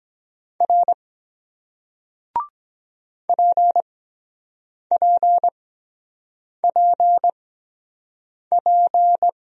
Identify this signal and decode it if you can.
Morse code, heard as a keyed tone, about 23 wpm, spaced as for 7 wpm, 700 Hz. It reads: R PPPP